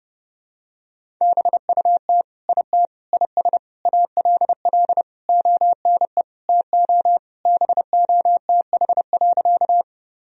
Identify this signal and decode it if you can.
Morse code, heard as a keyed tone, about 32 wpm, 700 Hz. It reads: BUT IT IS ALL ONE TO BOTH.
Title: Sentences from Top 1000 Words - Review Part 1 - 30wpm